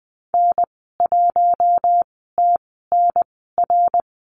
Morse code, heard as a keyed tone, about 19 wpm, spaced as for 14 wpm, 700 Hz. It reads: N1TNR
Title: 062 Lesson 11g - Letter h - Call Signs 20wpm